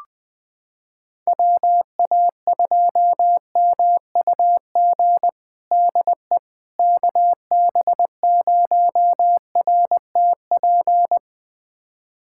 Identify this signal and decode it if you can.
Morse code, heard as a keyed tone, about 20 wpm, 700 Hz. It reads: WA2MUG DE KB0RTP
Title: USA Callsign DE Callsign - 6 Characters - 20wpm